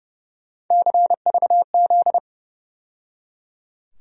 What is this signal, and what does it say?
Morse code, heard as a keyed tone, about 31 wpm, 700 Hz. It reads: CVZ
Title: USA Call Sign Postfixes - 3 Letters 30wpm